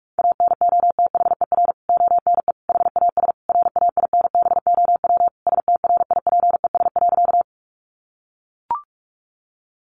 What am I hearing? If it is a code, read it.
Morse code, heard as a keyed tone, about 44 wpm, 700 Hz. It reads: ANOTHER ONE HAS RAINBOW STRIPES.
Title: Additional Easy Phrases 45wpm